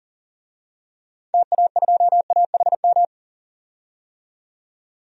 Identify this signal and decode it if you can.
Morse code, heard as a keyed tone, about 40 wpm, 700 Hz. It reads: TA2AHM
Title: DXCC-100 Call Signs - 6 Characters - 40wpm